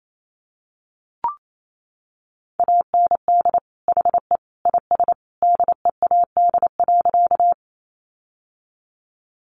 Morse code, heard as a keyed tone, about 28 wpm, 700 Hz. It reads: AND HE IS DEAD.
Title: Sentences from Top 1500 Words 28wpm